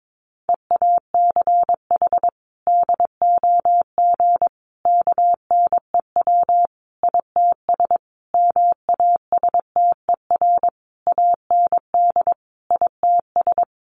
Morse code, heard as a keyed tone, about 22 wpm, 700 Hz. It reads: EACH DOG KNEW ITS MASTER AND ITS
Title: Sentences from Top 1300 Words 22wpm